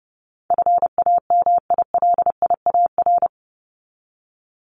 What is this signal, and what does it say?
Morse code, heard as a keyed tone, about 30 wpm, 700 Hz. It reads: FAMILIAR
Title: Top 751-1000 Words 30wpm